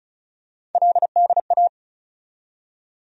Morse code, heard as a keyed tone, about 34 wpm, 700 Hz. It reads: LDA
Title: USA Call Sign Postfixes - 3 Letters 35wpm